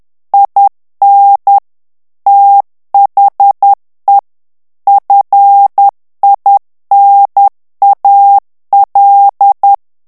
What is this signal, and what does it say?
Morse code, heard as a keyed tone, about 10 wpm, 800 Hz. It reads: IN THE FINAL
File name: ANGLAIS texte clair 3 vitesse 10 mots minute